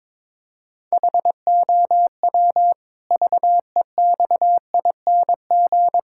Morse code, heard as a keyed tone, about 22 wpm, 700 Hz. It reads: HOW VEXING
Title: Pangram Challenge v2 - 22wpm